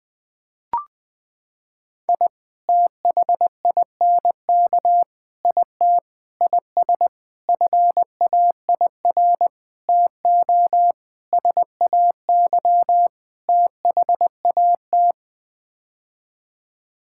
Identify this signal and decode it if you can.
Morse code, heard as a keyed tone, about 20 wpm, 700 Hz. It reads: I THINK IT IS FAIR TO SAY THAT
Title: Sets of 8 Words - 20wpm